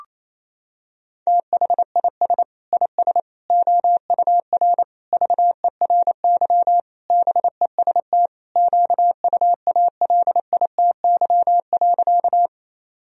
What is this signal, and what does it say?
Morse code, heard as a keyed tone, about 28 wpm, 700 Hz. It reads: THIS IS OUR VERY BEST QUALITY.
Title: Sentences from Top 1000 Words - Review Part 4 - 28wpm